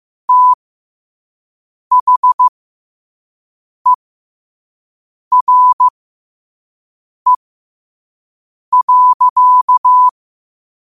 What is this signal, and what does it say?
Morse code, heard as a keyed tone, about 15 wpm, spaced as for 5 wpm, 1000 Hz. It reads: THERE.
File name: qotdfiles / 5wpm